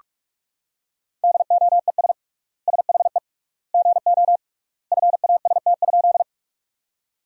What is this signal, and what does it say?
Morse code, heard as a keyed tone, about 42 wpm, 700 Hz. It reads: DOES SHE GO FAST?
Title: 3x Word-Spacing - Sentences from Top 400 Words - 45wpm